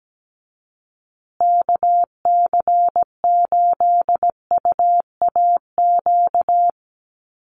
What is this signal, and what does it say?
Morse code, heard as a keyed tone, about 17 wpm, 700 Hz. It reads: KC8UAQ
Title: USA Call Signs Repeated 3x 17wpm